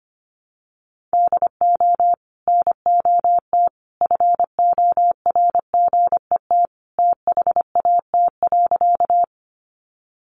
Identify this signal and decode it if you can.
Morse code, heard as a keyed tone, about 25 wpm, 700 Hz. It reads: DO NOT FORGET THAT.
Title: Sentences from Top 1200 Words 25wpm